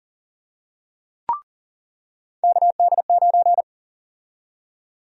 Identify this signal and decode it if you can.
Morse code, heard as a keyed tone, about 40 wpm, 700 Hz. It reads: KD9
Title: USA Call Sign Prefixes - 40wpm